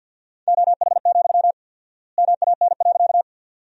Morse code, heard as a keyed tone, about 52 wpm, 700 Hz. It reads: OH, MAN.
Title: 4x Word-Spacing - Sentences from Top 500 Words - 50wpm